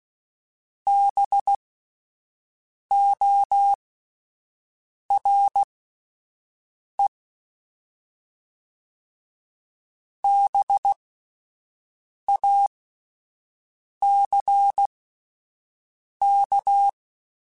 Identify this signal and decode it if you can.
Morse code, heard as a keyed tone, about 16 wpm, spaced as for 5 wpm, 800 Hz. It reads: BORE BACK
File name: Spaced Alphas / Spaced Alphas mp3s / Bs sp @10